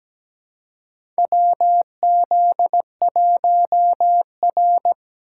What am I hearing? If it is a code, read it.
Morse code, heard as a keyed tone, about 17 wpm, 700 Hz. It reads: WZ1R
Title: USA Call Signs - 4 Characters - 3x Character Spacing - 17wpm